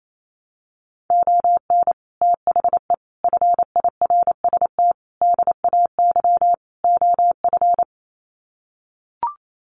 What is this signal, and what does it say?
Morse code, heard as a keyed tone, about 28 wpm, 700 Hz. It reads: ON THE FIRST DAY OF
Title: Sets of 5 Words Encore 28wpm